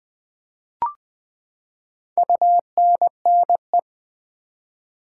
Morse code, heard as a keyed tone, about 20 wpm, 700 Hz. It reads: UNNE
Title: Build A Word - 4 Letters At A Time - 11-Letter Words - 20wpm